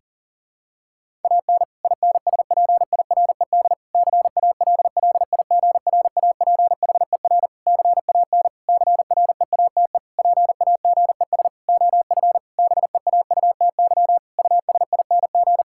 Morse code, heard as a keyed tone, about 37 wpm, 700 Hz. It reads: AN INSPIRED CALLIGRAPHER CAN CREATE PAGES OF BEAUTY USING
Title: Pangram Challenge v2 - 40wpm